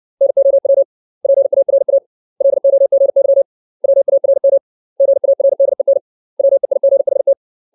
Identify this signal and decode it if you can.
Morse code, heard as a keyed tone, about 60 wpm, 550 Hz. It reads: NOW PARA LOGY WARM GARDEN WEIGHT
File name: 632 words / 632 Words 60WPM